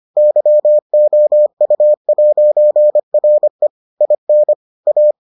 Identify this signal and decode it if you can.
Morse code, heard as a keyed tone, about 25 wpm, 600 Hz. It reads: YOU'RE IN A